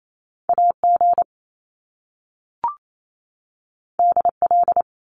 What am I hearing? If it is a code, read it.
Morse code, heard as a keyed tone, about 28 wpm, 700 Hz. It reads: AG DL
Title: Top Two Letter Combinations in Words 28wpm